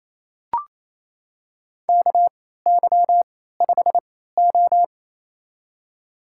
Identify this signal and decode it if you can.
Morse code, heard as a keyed tone, about 27 wpm, spaced as for 16 wpm, 700 Hz. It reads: KY5O
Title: USA Call Signs - 3x Character Spacing - 28wpm